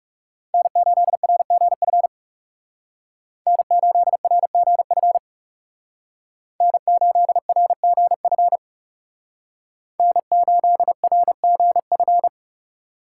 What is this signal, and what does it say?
Morse code, heard as a keyed tone, about 37 wpm, 700 Hz. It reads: N8RGF N8RGF N8RGF N8RGF
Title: Speed Racing - USA 1x3 Call Signs 30wpm